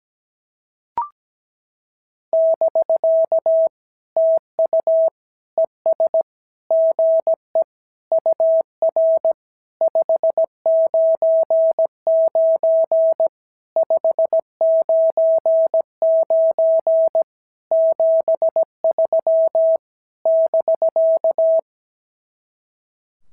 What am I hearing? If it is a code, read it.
Morse code, heard as a keyed tone, about 16 wpm, 650 Hz. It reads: <BK> TU ES GE UR 599 599 73 <BK>
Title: SOTA QSO Practice 17wpm